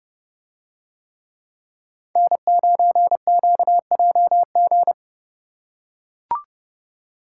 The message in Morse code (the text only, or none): N9QJG